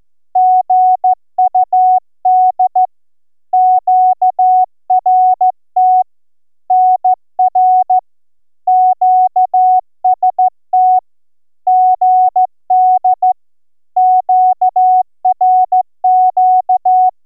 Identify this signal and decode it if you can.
Morse code, heard as a keyed tone, about 14 wpm, 740 Hz. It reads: GUD QRT NR QST GD QRQ